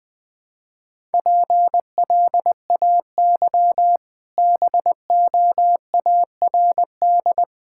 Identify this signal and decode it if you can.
Morse code, heard as a keyed tone, about 20 wpm, 700 Hz. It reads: PLAY BOARD
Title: Easy Sentences with Hints - 20wpm